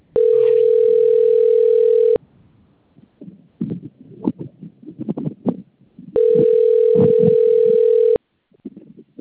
Ringback tone (North America), 440 + 480 Hz.
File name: out- - - - - .74445